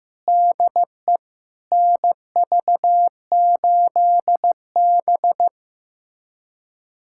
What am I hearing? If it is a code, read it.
Morse code, heard as a keyed tone, about 15 wpm, 700 Hz. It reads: DE NV8B
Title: USA Callsign DE Callsign - 4 Characters - 15wpm